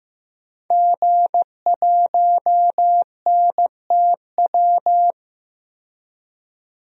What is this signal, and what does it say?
Morse code, heard as a keyed tone, about 15 wpm, 700 Hz. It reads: G1NTW
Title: DXCC-100 Call Signs - 5 Characters - 15wpm